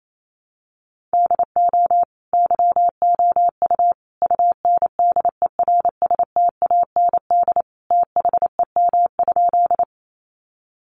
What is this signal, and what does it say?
Morse code, heard as a keyed tone, about 28 wpm, 700 Hz. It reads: DO YOU UNDERSTAND THEM?
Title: Sentences from Top 600 Words 28wpm